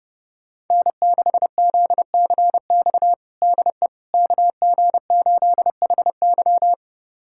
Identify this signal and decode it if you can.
Morse code, heard as a keyed tone, about 30 wpm, 700 Hz. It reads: N6ZCX DE KG8HY
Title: USA Callsign DE Callsign 30wpm